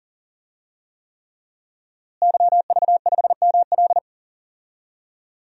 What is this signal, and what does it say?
Morse code, heard as a keyed tone, about 38 wpm, 700 Hz. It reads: YV5ML
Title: DXCC-100 Call Signs - 5 Characters - Repeated 3x - 40wpm